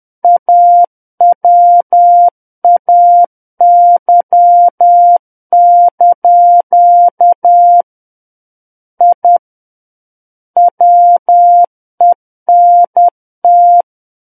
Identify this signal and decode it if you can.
Morse code, heard as a keyed tone, about 10 wpm, 700 Hz. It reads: AWAY) I WENT
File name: Book One / 01-Chapter 1 - 10 WPM